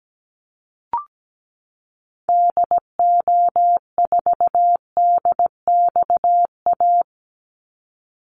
Morse code, heard as a keyed tone, about 17 wpm, 700 Hz. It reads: DO4DXA
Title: Call Signs Repeated 3x - 17wpm